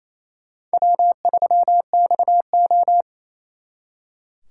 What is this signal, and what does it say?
Morse code, heard as a keyed tone, about 35 wpm, 700 Hz. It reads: W3XO
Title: Speed Racing - USA 1x2 Call Signs 28wpm